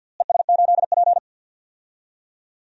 Morse code, heard as a keyed tone, about 48 wpm, 700 Hz. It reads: ES8P
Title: DXCC-100 Call Signs - 4 Characters - Repeated 2x - 50wpm